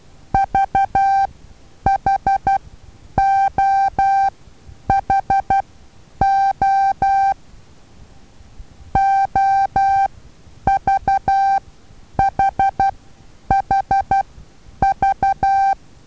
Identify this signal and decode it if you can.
Morse code, heard as a keyed tone, about 12 wpm, spaced as for 9 wpm, 780 Hz. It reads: VHOHO OVHHV
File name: Taviro HA6VA A